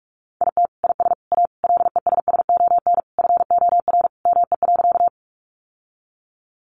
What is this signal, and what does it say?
Morse code, heard as a keyed tone, about 44 wpm, 700 Hz. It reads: IT IS A LESSON FOR ME.
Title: Sentences from Top 1200 Words 45wpm